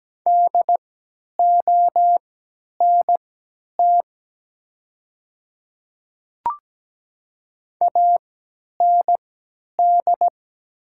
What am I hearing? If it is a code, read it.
Morse code, heard as a keyed tone, about 17 wpm, spaced as for 10 wpm, 700 Hz. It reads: DONT AND
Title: 3x Character Spacing - Top 300 Words - 17wpm